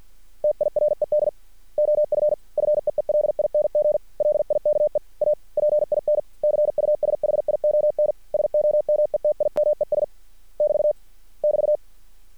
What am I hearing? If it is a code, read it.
Morse code, heard as a keyed tone, about 50 wpm, 600 Hz. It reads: TIRED OF FEELING LIKE A PIN CUSHION SOMETIMES <BT> <BT>
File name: archived / CRSnet